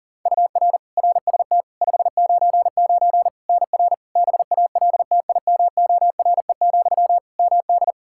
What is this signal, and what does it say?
Morse code, heard as a keyed tone, about 40 wpm, 700 Hz. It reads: UR RST 599 NR BALTIMORE, MD
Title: Practice QSO - Exchange 1 - Beginning - 40wpm